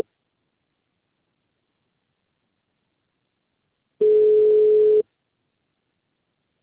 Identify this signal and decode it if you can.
Ringback tone (Europe, 425 Hz).